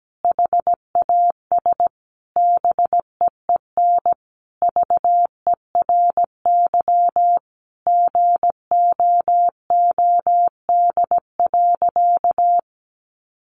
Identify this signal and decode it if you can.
Morse code, heard as a keyed tone, about 17 wpm, 700 Hz. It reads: HAS BEEN VERY GOOD.
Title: Sentences from Top 1400 Words 17wpm